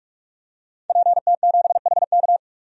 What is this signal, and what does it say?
Morse code, heard as a keyed tone, about 48 wpm, 700 Hz. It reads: PT7HK
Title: DXCC-100 Call Signs - 5 Characters - 45wpm